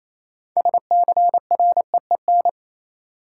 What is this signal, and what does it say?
Morse code, heard as a keyed tone, about 27 wpm, 700 Hz. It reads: SCREEN
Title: Sentences from Top 1000 Words - Review Part 6 - New Word Drill - 28wpm